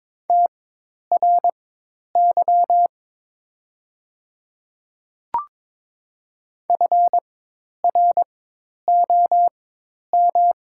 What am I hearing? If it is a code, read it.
Morse code, heard as a keyed tone, about 22 wpm, spaced as for 10 wpm, 700 Hz. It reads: TRY FROM
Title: 4x Character Spacing - Top 300 Words - 22wpm